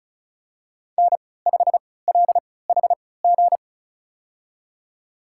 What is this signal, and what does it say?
Morse code, heard as a keyed tone, about 34 wpm, spaced as for 20 wpm, 700 Hz. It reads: N5LHG
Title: USA Call Signs - 3x Character Spacing - 35wpm